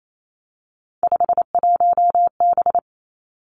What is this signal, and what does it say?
Morse code, heard as a keyed tone, about 28 wpm, 690 Hz. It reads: 51B
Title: Three Challenging Letter-Number Combinations 28wpm